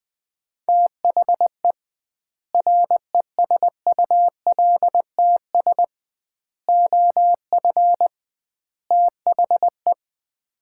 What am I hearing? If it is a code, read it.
Morse code, heard as a keyed tone, about 20 wpm, 700 Hz. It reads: THE RESULTS OF THE
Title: 2x Word-Spacing - Sets of 4 Words - 20wpm